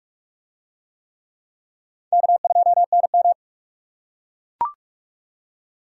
Morse code, heard as a keyed tone, about 42 wpm, 700 Hz. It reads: K2NM